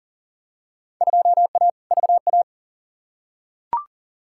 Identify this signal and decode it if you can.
Morse code, heard as a keyed tone, about 40 wpm, 700 Hz. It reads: 2A VA
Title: Field Day Exchanges - 40wpm